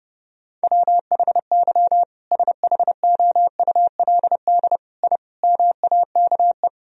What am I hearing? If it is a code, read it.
Morse code, heard as a keyed tone, about 32 wpm, 700 Hz. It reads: WHY SHOULD I MAKE
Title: Sentences from Top 1000 Words - Review Part 1 - 30wpm